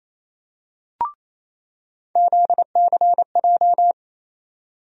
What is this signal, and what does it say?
Morse code, heard as a keyed tone, about 28 wpm, 700 Hz. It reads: ZCJ
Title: USA Call Sign Postfixes - 3 Letters 28wpm